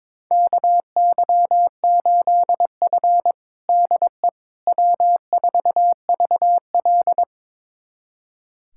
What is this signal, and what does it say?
Morse code, heard as a keyed tone, about 22 wpm, 700 Hz. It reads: KY8F DE W4VL
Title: USA Callsign DE Callsign - 4 Characters - 22wpm